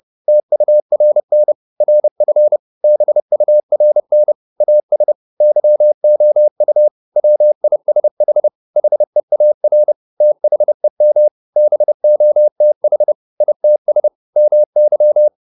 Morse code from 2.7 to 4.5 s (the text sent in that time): BURN